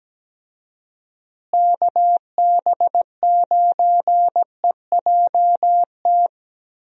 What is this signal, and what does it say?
Morse code, heard as a keyed tone, about 17 wpm, 700 Hz. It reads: KB9EJT
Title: USA Call Signs - 6 Characters - Repeated 3x - 17wpm